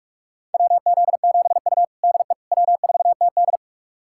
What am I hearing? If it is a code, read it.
Morse code, heard as a keyed tone, about 46 wpm, 700 Hz. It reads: WZ7U DE W4TD
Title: USA Callsign DE Callsign - 4 Characters - 45wpm